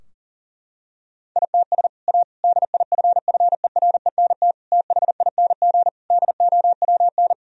Morse code, heard as a keyed tone, about 40 wpm, 700 Hz. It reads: ITS A DIFFERENT THING DOWN
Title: Sentences from Top 1000 Words - Review Part 3 - 40wpm